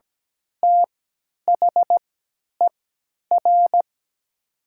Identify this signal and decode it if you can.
Morse code, heard as a keyed tone, about 17 wpm, spaced as for 10 wpm, 700 Hz. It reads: THER
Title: 3x Character Spacing - Top 300 Words - 17wpm